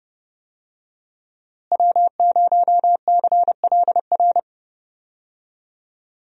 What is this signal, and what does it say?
Morse code, heard as a keyed tone, about 30 wpm, 700 Hz. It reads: W0CLR
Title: USA Call Signs - 5 Characters - Repeated 3x - 30wpm